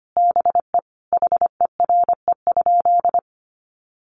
Morse code, heard as a keyed tone, about 25 wpm, 700 Hz. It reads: BE HERE?
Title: Sentences from Top 1100 Words 25wpm